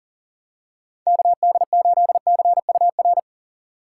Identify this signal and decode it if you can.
Morse code, heard as a keyed tone, about 40 wpm, 700 Hz. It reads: KD8CUR
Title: USA Call Signs - 6 Characters - 6x Character Spacing - 40wpm